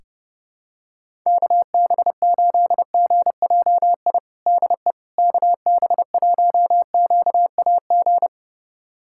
Morse code, heard as a keyed tone, about 30 wpm, 700 Hz. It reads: KB8GJI DE KB1QAG
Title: USA Callsign DE Callsign 30wpm